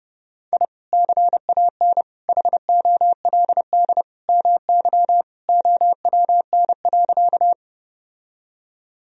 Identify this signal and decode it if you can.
Morse code, heard as a keyed tone, about 30 wpm, 700 Hz. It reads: I CAN HOLD MY OWN.
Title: Sentences from Top 1000 Words - Review Part 2 - 30wpm